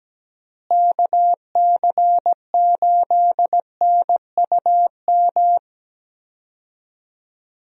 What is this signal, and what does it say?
Morse code, heard as a keyed tone, about 17 wpm, 700 Hz. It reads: KC8NUM
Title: USA Call Signs - 6 Characters - Repeated 3x - 17wpm